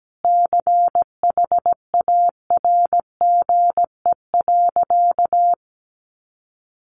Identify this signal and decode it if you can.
Morse code, heard as a keyed tone, about 17 wpm, 700 Hz. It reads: CHARGE.
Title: Sentences from Top 1300 Words 17wpm